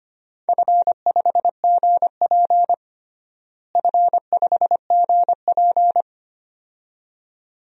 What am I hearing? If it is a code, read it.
Morse code, heard as a keyed tone, about 25 wpm, 700 Hz. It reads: F5GP F5GP
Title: DXCC-100 Call Signs - 4 Characters - Repeated 2x - 25wpm